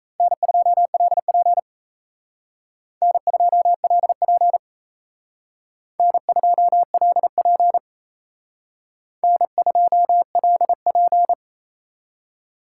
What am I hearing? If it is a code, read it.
Morse code, heard as a keyed tone, about 36 wpm, 700 Hz. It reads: N2LP N2LP N2LP N2LP